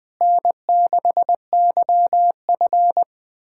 Morse code, heard as a keyed tone, about 20 wpm, 700 Hz. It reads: N6YF
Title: USA Callsign DE Callsign - 4 Characters - 20wpm